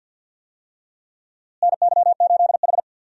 Morse code, heard as a keyed tone, about 53 wpm, 700 Hz. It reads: NY8H
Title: USA Call Signs - 4 Characters - Repeated 3x - 50wpm